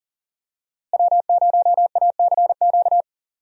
Morse code, heard as a keyed tone, about 39 wpm, 700 Hz. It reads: W0ACQ
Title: USA Call Signs - 5 Characters - Repeated 3x - 40wpm